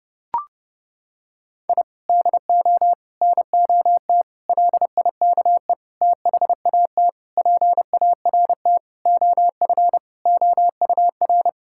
Morse code, heard as a keyed tone, about 32 wpm, 700 Hz. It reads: I DO NOT LIKE THAT PART OF OUR
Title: Sentences from Top 1000 Words - Review Part 6 - 30wpm